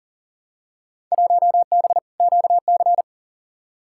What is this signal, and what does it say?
Morse code, heard as a keyed tone, about 40 wpm, 700 Hz. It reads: 1B QC
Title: Field Day Exchanges Encore - 40wpm